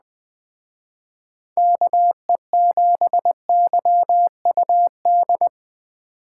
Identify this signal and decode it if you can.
Morse code, heard as a keyed tone, about 22 wpm, 700 Hz. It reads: KE7YUD